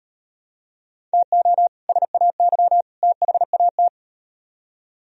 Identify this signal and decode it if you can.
Morse code, heard as a keyed tone, about 38 wpm, 700 Hz. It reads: TO SAY THAT